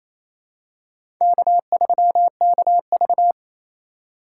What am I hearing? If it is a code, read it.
Morse code, heard as a keyed tone, about 27 wpm, 700 Hz. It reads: K3KV
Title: USA Call Signs - 2x Character Spacing - 28wpm